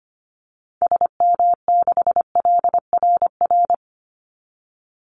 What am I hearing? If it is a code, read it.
Morse code, heard as a keyed tone, about 25 wpm, 700 Hz. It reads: SM6LRR
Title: Call Signs Repeated 3x - 25wpm